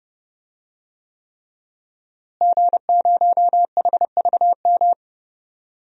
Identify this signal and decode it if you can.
Morse code, heard as a keyed tone, about 31 wpm, 700 Hz. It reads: G0HVM
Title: DXCC-100 Call Signs - 5 Characters - Repeated 3x - 30wpm